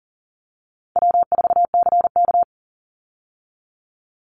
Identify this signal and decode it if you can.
Morse code, heard as a keyed tone, about 38 wpm, 700 Hz. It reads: W4CK
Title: Easy Call Signs - 40wpm